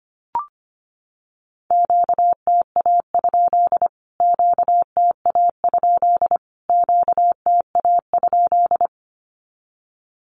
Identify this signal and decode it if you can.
Morse code, heard as a keyed tone, about 25 wpm, 700 Hz. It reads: QTA? QTA? QTA?
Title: Q-Codes Repeated 3x 25wpm